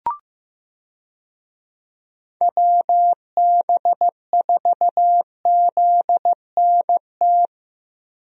Morse code, heard as a keyed tone, about 15 wpm, 700 Hz. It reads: WB4ZNT
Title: USA Call Signs - 6 Characters - Repeated 2x - 15wpm